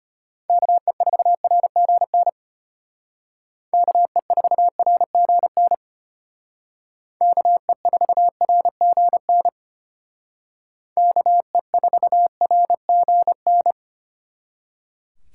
Morse code, heard as a keyed tone, about 33 wpm, 700 Hz. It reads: KE4RGN KE4RGN KE4RGN KE4RGN